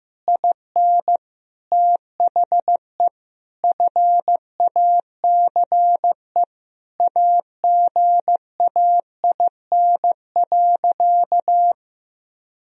Morse code, heard as a keyed tone, about 16 wpm, 700 Hz. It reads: IN THE FACE AGAIN.
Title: Sentences from Top 1000 Words - Review Part 2 - 15wpm